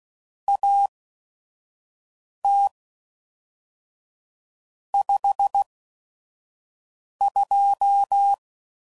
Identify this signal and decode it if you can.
Morse code, heard as a keyed tone, about 16 wpm, spaced as for 5 wpm, 800 Hz. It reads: AT 52
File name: CW Code Groups @5WPM / Weather Sets / Weather 4 17